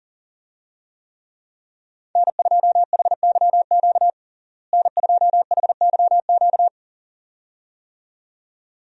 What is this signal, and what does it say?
Morse code, heard as a keyed tone, about 37 wpm, 700 Hz. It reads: N2HYQ N2HYQ
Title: USA Call Signs - 5 Characters - Repeated 2x - 40wpm